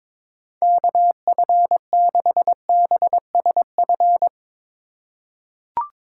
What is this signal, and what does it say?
Morse code, heard as a keyed tone, about 21 wpm, 700 Hz. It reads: KF6BSF
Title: USA Call Signs - 6 Characters - 2x Character Spacing - 22wpm